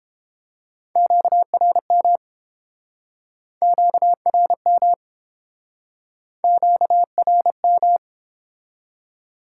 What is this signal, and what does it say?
Morse code, heard as a keyed tone, about 30 wpm, 700 Hz. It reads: QRM QRM QRM